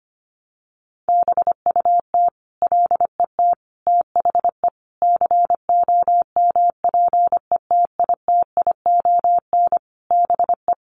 Morse code, heard as a keyed tone, about 25 wpm, 700 Hz. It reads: BUT LET THE COMPETITION BE
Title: Sentences from Top 1300 Words 25wpm